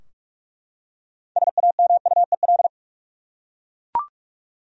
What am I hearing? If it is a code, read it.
Morse code, heard as a keyed tone, about 43 wpm, 700 Hz. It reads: SAMUEL
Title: USA Common Names - 45wpm